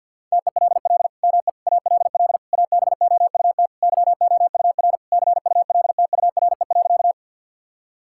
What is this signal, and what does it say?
Morse code, heard as a keyed tone, about 49 wpm, 700 Hz. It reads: TELL ME ALL ABOUT YOUR CULTURE.
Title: Sentences from Top 1000 Words - Review Part 6 - 50wpm